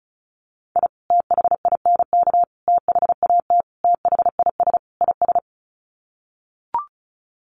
Morse code, heard as a keyed tone, about 34 wpm, 700 Hz. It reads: I THINK THAT THIS IS
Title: Sets of 5 Words Encore2 35wpm